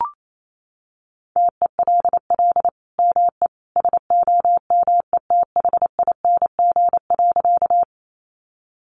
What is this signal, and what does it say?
Morse code, heard as a keyed tone, about 28 wpm, 700 Hz. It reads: TELL ME SOMETHING.
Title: Sentences from Top 600 Words 28wpm